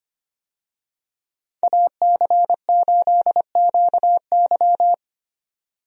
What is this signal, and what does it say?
Morse code, heard as a keyed tone, about 25 wpm, 700 Hz. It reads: AC8QY
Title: USA Call Signs - 5 Characters - 25wpm